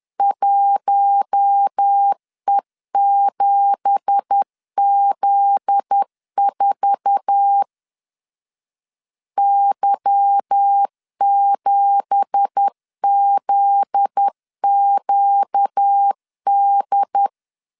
Morse code, heard as a keyed tone, about 10 wpm, 800 Hz. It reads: 1E7Z4 Y7ZQD
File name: LEZIONI / 9 2867 10wpm 2733